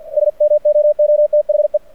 Morse code, heard as a keyed tone, about 49 wpm, 600 Hz. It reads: SMOOTHE
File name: archived / CRSnet